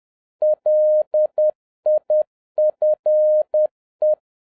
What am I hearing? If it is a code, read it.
Morse code, heard as a keyed tone, about 10 wpm, 610 Hz. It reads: LIFE
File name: 2011-07-mp3 / n56 BACK FROM A BIG ACCIDENT 10wpm